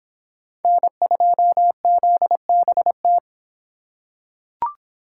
Morse code, heard as a keyed tone, about 26 wpm, 700 Hz. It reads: N2ZBT